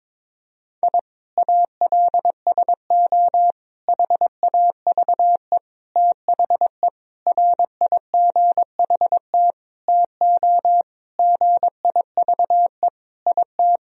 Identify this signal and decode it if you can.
Morse code, heard as a keyed tone, about 22 wpm, 700 Hz. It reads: I ALSO HAVE THE RIGHT TO GIVE IT
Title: Sentences from Top 1000 Words - Review Part 1 - 22wpm